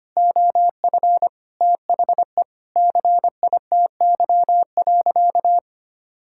Morse code, heard as a keyed tone, about 25 wpm, 700 Hz. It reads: OF THE CITY.
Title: Sentences from Top 1000 Words - Review Part 5 - 25wpm